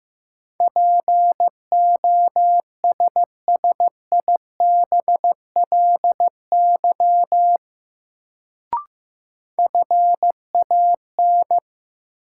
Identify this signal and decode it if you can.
Morse code, heard as a keyed tone, about 15 wpm, 700 Hz. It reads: POSSIBLY FAN
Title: Sentences from Top 1300 Words - New Word Drill Alt - 15wpm